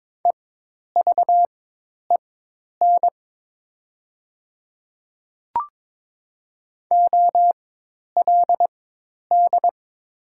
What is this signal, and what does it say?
Morse code, heard as a keyed tone, about 22 wpm, spaced as for 10 wpm, 700 Hz. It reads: EVEN OLD
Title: 4x Character Spacing - Top 300 Words - 22wpm